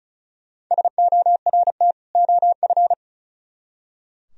Morse code, heard as a keyed tone, about 42 wpm, 700 Hz. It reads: SORT OF
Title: Speed-Racing - Sets of 2 Words 35wpm